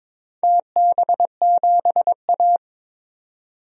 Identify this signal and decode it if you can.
Morse code, heard as a keyed tone, about 22 wpm, 700 Hz. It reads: TB7A
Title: DXCC-100 Call Signs - 4 Characters - 22wpm